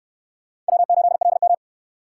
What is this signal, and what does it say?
Morse code, heard as a keyed tone, about 64 wpm, 700 Hz. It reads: W8LM